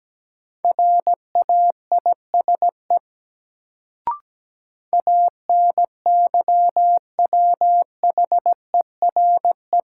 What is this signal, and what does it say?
Morse code, heard as a keyed tone, about 17 wpm, 700 Hz. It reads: RAISE ANYWHERE
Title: Sentences from Top 1400 Words - New Word Drill Alt - 17wpm